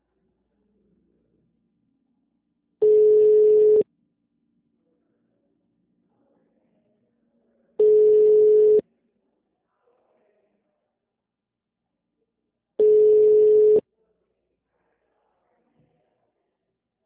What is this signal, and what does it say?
Ringback tone (Europe, 425 Hz).
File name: out- -2011- - - .34544